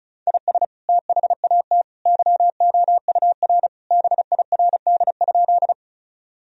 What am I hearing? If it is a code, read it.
Morse code, heard as a keyed tone, about 36 wpm, 700 Hz. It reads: IS THAT YOUR BIRD?
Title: Sentences from Top 1000 Words - Review Part 2 - 35wpm